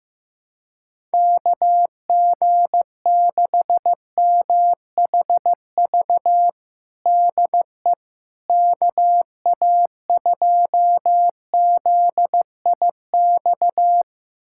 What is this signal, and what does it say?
Morse code, heard as a keyed tone, about 15 wpm, 700 Hz. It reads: KG6MHV DE KA2ZIX
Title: USA Callsign DE Callsign 15wpm